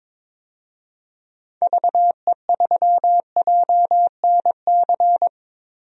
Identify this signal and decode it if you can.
Morse code, heard as a keyed tone, about 21 wpm, 700 Hz. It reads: VE3JNC
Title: DXCC-100 Call Signs - 6 Characters - Repeated 2x - 22wpm